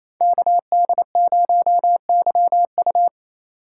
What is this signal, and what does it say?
Morse code, heard as a keyed tone, about 34 wpm, 700 Hz. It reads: KD0YU
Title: Speed Racing - USA 2x2 Call Signs 28wpm